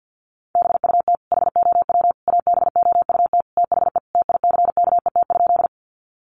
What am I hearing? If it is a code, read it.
Morse code, heard as a keyed tone, about 50 wpm, 700 Hz. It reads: BUT HOW ABOUT THE TICKET?
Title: Sentences from Top 1400 Words 50wpm